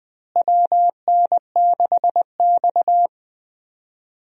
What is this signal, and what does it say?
Morse code, heard as a keyed tone, about 21 wpm, 700 Hz. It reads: WN6X
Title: USA Call Signs - 4 Characters - Repeated 3x - 20wpm